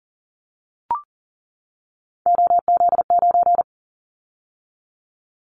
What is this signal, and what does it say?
Morse code, heard as a keyed tone, about 38 wpm, 700 Hz. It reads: OZ9
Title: Callsign Prefixes for Top 100 DXCC - 40wpm